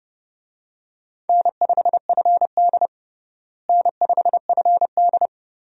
Morse code, heard as a keyed tone, about 30 wpm, 700 Hz. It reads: N5FD N5FD
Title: USA Call Signs - 4 Characters - Repeated 2x - 30wpm